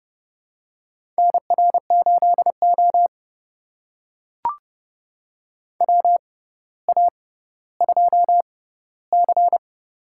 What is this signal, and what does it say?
Morse code, heard as a keyed tone, about 29 wpm, spaced as for 15 wpm, 700 Hz. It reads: NR8O WA2C
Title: USA Call Signs - 4 Characters - 6x Character Spacing - 30wpm